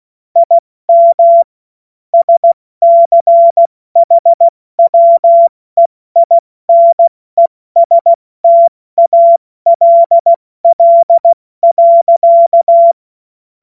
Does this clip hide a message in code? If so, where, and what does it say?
Yes: Morse code from 0.4 to 12.9 s, heard as a keyed tone, about 16 wpm, 680 Hz. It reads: IM SCHWEINESTALL.